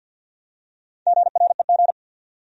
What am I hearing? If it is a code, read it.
Morse code, heard as a keyed tone, about 50 wpm, 700 Hz. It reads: GREG
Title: USA Common Names - Max 5 Letters - 50wpm